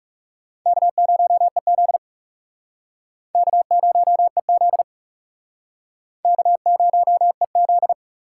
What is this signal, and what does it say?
Morse code, heard as a keyed tone, about 37 wpm, 700 Hz. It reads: K0EZ K0EZ K0EZ
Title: Speed Racing - USA 1x2 Call Signs 30wpm